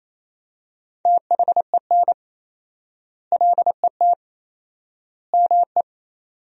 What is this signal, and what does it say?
Morse code, heard as a keyed tone, about 28 wpm, 700 Hz. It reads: THEN LET ME
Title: 4x Word-Spacing - Sentences from Top 300 Words - 28wpm